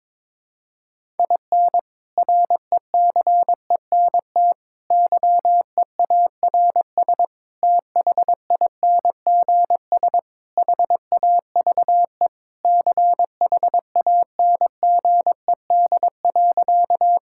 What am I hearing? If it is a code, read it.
Morse code, heard as a keyed tone, about 22 wpm, 700 Hz. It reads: IN RECENT YEARS THINGS HAVE CHANGED.
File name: practice / 22wpm / Sentences with 3-Word Phrases - 22wpm